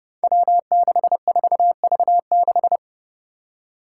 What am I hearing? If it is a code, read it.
Morse code, heard as a keyed tone, about 30 wpm, 700 Hz. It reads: W64V6
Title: Five Challenging Letter-Number Combinations 30wpm